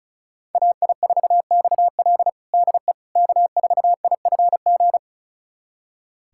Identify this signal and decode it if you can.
Morse code, heard as a keyed tone, about 36 wpm, 700 Hz. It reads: AI4XL DE K4IFG
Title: USA Callsign DE Callsign - 5 Characters - 35wpm